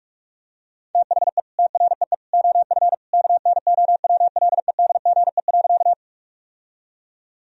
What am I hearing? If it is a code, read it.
Morse code, heard as a keyed tone, about 41 wpm, 700 Hz. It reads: THE TREE OF KNOWLEDGE.
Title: Sentences from Top 1000 Words - Review Part 3 - 45wpm